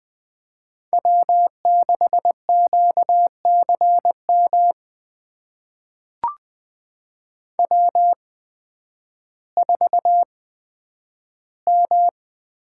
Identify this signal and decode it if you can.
Morse code, heard as a keyed tone, about 20 wpm, spaced as for 11 wpm, 700 Hz. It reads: W6QCM W4M
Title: USA Call Signs - 8x Character Spacing - 20wpm